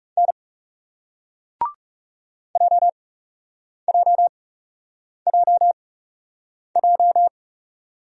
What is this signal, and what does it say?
Morse code, heard as a keyed tone, about 41 wpm, spaced as for 8 wpm, 700 Hz. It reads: N JJJJ